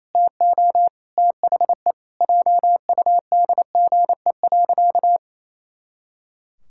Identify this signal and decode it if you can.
Morse code, heard as a keyed tone, about 28 wpm, 700 Hz. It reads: TO THE JUDGE.
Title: Pangram Challenge v2 - 28wpm